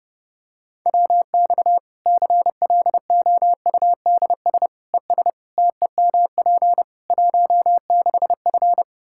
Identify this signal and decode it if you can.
Morse code, heard as a keyed tone, about 31 wpm, 700 Hz. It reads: WX CLOUDS ES TEMP 16F
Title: Practice QSO - Exchange 2 - Info - 30wpm